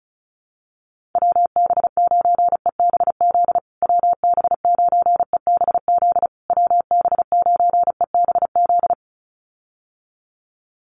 Morse code, heard as a keyed tone, about 34 wpm, 700 Hz. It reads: WB9EBZ WB9EBZ WB9EBZ
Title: USA Call Signs Repeated 3x 35wpm